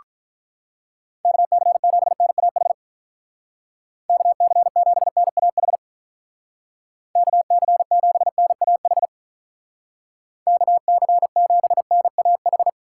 Morse code, heard as a keyed tone, about 46 wpm, 700 Hz. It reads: KC7NAH KC7NAH KC7NAH KC7NAH